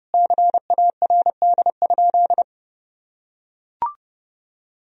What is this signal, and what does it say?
Morse code, heard as a keyed tone, about 30 wpm, 700 Hz. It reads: CARD?
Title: Sentences from Top 1000 Words - Review Part 4 - 30wpm